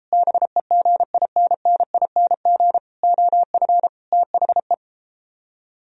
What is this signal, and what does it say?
Morse code, heard as a keyed tone, about 38 wpm, 700 Hz. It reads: BEGINNING OF THE